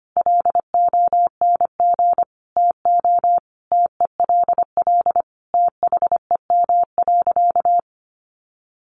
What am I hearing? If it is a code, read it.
Morse code, heard as a keyed tone, about 25 wpm, 700 Hz. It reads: LONG TO TELL THEM.
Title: Sentences from Top 700 Words 25wpm